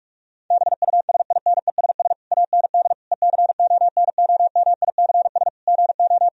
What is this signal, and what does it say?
Morse code, heard as a keyed tone, about 46 wpm, 700 Hz. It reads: BUSINESS AND ECONOMICS GO
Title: Sentences from Top 1000 Words - Review Part 3 - 45wpm